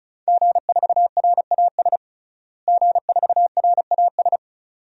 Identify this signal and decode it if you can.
Morse code, heard as a keyed tone, about 34 wpm, 700 Hz. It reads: G4RAS G4RAS
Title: DXCC-100 Call Signs - 5 Characters - Repeated 2x - 35wpm